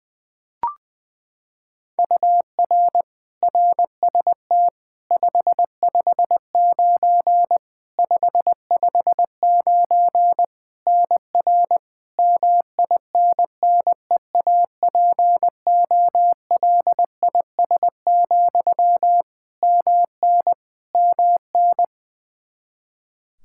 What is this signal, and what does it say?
Morse code, heard as a keyed tone, about 21 wpm, 700 Hz. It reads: UR RST 559 559 NR MINNEAPOLIS, MN MN
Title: Practice QSO - Exchange 1 - Beginning - 20wpm